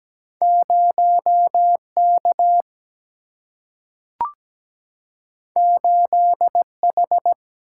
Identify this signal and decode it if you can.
Morse code, heard as a keyed tone, about 17 wpm, 700 Hz. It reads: 0 K 8 H
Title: Single-Letter Single-Number 17wpm